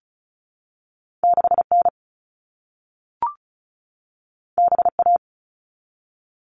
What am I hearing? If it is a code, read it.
Morse code, heard as a keyed tone, about 36 wpm, 700 Hz. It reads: 6N BA
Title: All Cut Numbers - 2-Digit 35wpm